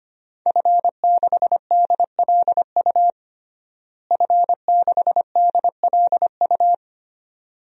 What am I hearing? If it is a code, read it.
Morse code, heard as a keyed tone, about 25 wpm, 700 Hz. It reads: F6DLU F6DLU
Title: DXCC-100 Call Signs - 5 Characters - Repeated 2x - 25wpm